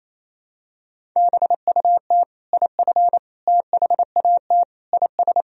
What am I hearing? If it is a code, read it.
Morse code, heard as a keyed tone, about 28 wpm, 700 Hz. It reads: BUT IF THAT IS
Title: Sentences from Top 200 Words and Abbreviations 28wpm